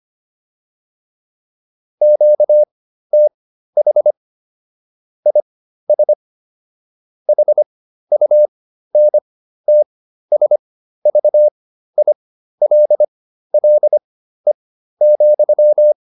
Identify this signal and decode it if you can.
Morse code, heard as a keyed tone, about 25 wpm, spaced as for 13 wpm, 600 Hz. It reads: QTH IS HUNTSVILLE,
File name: QSO 305 13